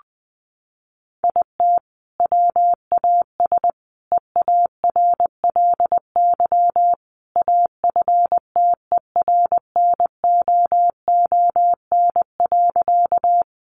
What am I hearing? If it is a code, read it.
Morse code, heard as a keyed tone, about 20 wpm, 700 Hz. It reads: IT WAS EARLY AFTERNOON.
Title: Sentences from Top 1500 Words 20wpm